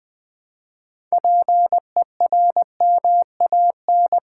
Morse code, heard as a keyed tone, about 20 wpm, 700 Hz. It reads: PERMAN
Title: Build A Word - 3 Letters At A Time - 9-Letter Words - 20wpm